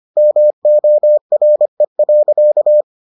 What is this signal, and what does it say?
Morse code, heard as a keyed tone, about 25 wpm, 600 Hz. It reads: MORE.